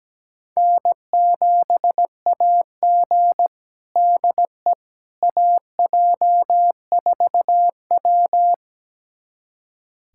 Morse code, heard as a keyed tone, about 17 wpm, 700 Hz. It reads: N7AG DE AJ4W
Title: USA Callsign DE Callsign - 4 Characters - 17wpm